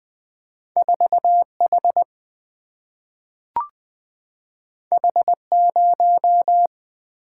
Morse code, heard as a keyed tone, about 20 wpm, 700 Hz. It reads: 4H H0
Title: Single-Letter Single-Number 20wpm